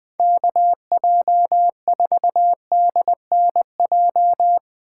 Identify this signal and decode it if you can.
Morse code, heard as a keyed tone, about 20 wpm, 700 Hz. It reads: KJ4DNJ
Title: Speed Racing - USA 2x3 Call Signs 17wpm